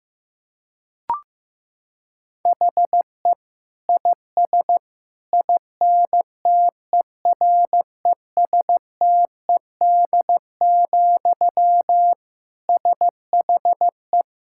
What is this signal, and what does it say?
Morse code, heard as a keyed tone, about 16 wpm, 700 Hz. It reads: HE IS INTERESTED, SHE
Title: Sentences from Top 1000 Words - Review Part 6 - 15wpm